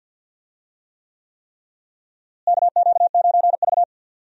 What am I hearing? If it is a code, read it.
Morse code, heard as a keyed tone, about 48 wpm, 700 Hz. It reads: KQ9V
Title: USA Call Signs - 4 Characters - Repeated 3x - 50wpm